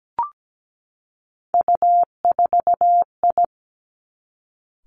Morse code, heard as a keyed tone, about 17 wpm, 700 Hz. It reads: U4I
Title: Three Challenging Letter-Number Combinations 17wpm